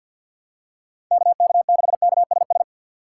Morse code, heard as a keyed tone, about 47 wpm, 700 Hz. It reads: KK6KSS
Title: USA Call Signs - 4x Character Spacing - 50wpm